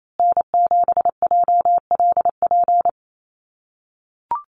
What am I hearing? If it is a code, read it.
Morse code, heard as a keyed tone, about 28 wpm, 700 Hz. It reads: N7JLP
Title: USA Call Signs Repeated 2x 28wpm